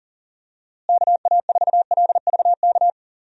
Morse code, heard as a keyed tone, about 38 wpm, 700 Hz. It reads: KA4LVK
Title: USA Call Signs - 6 Characters - Repeated 3x - 40wpm